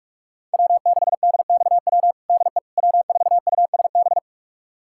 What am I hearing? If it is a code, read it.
Morse code, heard as a keyed tone, about 41 wpm, 700 Hz. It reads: W6DXW DE W4USB
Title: USA Callsign DE Callsign - 5 Characters - 45wpm